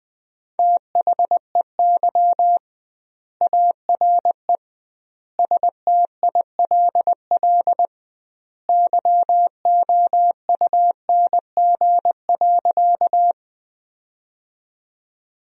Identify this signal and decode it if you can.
Morse code, heard as a keyed tone, about 20 wpm, 700 Hz. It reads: THEY ARE STILL YOUNG.
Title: 2x Word-Spacing - Sentences from Top 400 Words - 20wpm